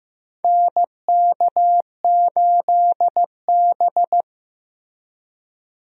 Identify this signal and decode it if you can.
Morse code, heard as a keyed tone, about 16 wpm, 700 Hz. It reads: NK8B
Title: USA Call Signs - 4 Characters - Repeated 2x - 15wpm